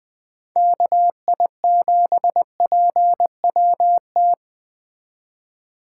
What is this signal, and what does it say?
Morse code, heard as a keyed tone, about 20 wpm, 700 Hz. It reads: KI7PWT